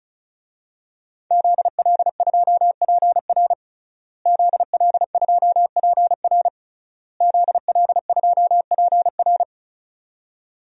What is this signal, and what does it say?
Morse code, heard as a keyed tone, about 36 wpm, 700 Hz. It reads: ZL2PR ZL2PR ZL2PR
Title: DXCC-100 Call Signs - 5 Characters - Repeated 3x - 35wpm